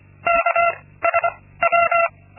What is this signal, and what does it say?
Morse code, heard as a keyed tone, about 25 wpm, 680 Hz. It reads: K S W
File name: Linksmakalnis-2